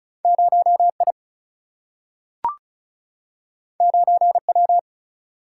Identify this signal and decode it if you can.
Morse code, heard as a keyed tone, about 36 wpm, 700 Hz. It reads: 0 I 9 W
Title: Single-Letter Single-Number 35wpm